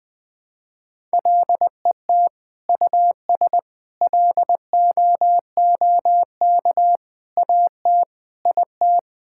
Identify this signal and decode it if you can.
Morse code, heard as a keyed tone, about 20 wpm, 700 Hz. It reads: LET US LOOK AT IT
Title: Sentences from Top 1000 Words - Review Part 6 - 20wpm